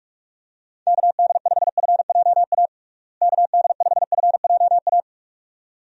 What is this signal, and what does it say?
Morse code, heard as a keyed tone, about 46 wpm, 700 Hz. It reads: KD5FJA KD5FJA
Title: USA Call Signs - 6 Characters - Repeated 2x - 45wpm